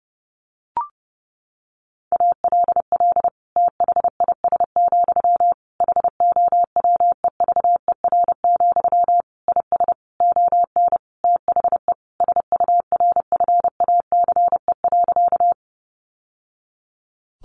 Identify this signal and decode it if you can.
Morse code, heard as a keyed tone, about 30 wpm, 700 Hz. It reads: ALL THIS, HOWEVER, IS ON THE SURFACE.
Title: Sentences from Top 600 Words 30wpm